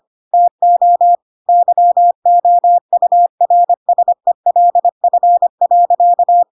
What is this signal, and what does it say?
Morse code, heard as a keyed tone, about 25 wpm, 700 Hz. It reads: TO YOURSELF.
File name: Pinocchio / Pinocchio ch5 25 25 ik1hge